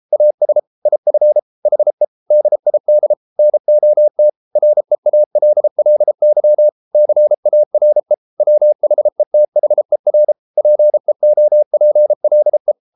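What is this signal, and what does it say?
Morse code, heard as a keyed tone, about 33 wpm, 600 Hz. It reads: AS IF HE DID NOT REALLY CARE WHETHER PEOPLE
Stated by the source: Morse code practice at 32 wpm from CW med Gnister, reading Ken Follett's novel Never in English.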